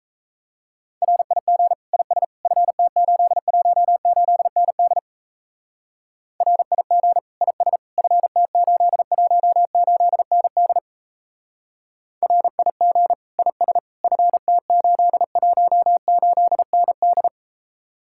Morse code, heard as a keyed tone, about 35 wpm, 700 Hz. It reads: RIG IS FT818ND RIG IS FT818ND RIG IS FT818ND
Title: Speed-Racing Common QSO Phrases Encore - 28wpm